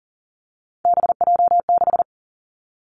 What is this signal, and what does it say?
Morse code, heard as a keyed tone, about 41 wpm, 700 Hz. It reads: BJ6
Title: Three Challenging Letter-Number Combinations 40wpm